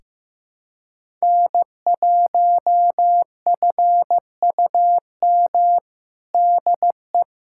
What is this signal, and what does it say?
Morse code, heard as a keyed tone, about 16 wpm, 700 Hz. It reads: N1FUM DE
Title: USA Callsign DE Callsign - 5 Characters - 15wpm